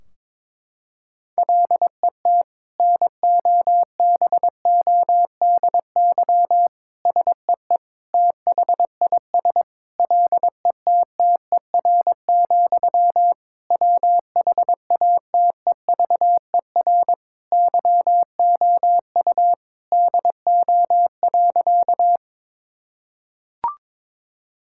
Morse code, heard as a keyed tone, about 22 wpm, 700 Hz. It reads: LET NOBODY SEE THIS LETTER, WHATEVER YOU DO.
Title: Sentences from Top 1600 Words 22wpm